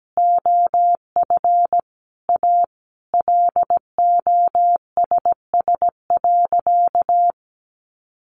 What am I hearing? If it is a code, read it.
Morse code, heard as a keyed tone, about 17 wpm, 700 Hz. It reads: OF A LOSS.
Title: Sentences from Top 1400 Words 17wpm